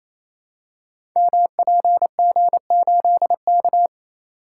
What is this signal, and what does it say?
Morse code, heard as a keyed tone, about 29 wpm, 700 Hz. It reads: MPG8K
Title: Code Groups - 5 Letter:Numbers - 28wpm